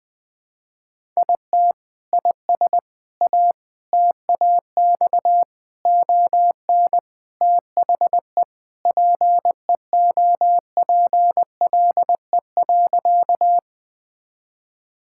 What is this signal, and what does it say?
Morse code, heard as a keyed tone, about 20 wpm, 700 Hz. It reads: IT IS A TAX ON THE PEOPLE.
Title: Sentences from Top 1000 Words - Review Part 5 - 20wpm